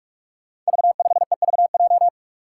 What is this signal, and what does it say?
Morse code, heard as a keyed tone, about 42 wpm, 700 Hz. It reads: V5EVJ
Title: Five Challenging Letter-Number Combinations 45wpm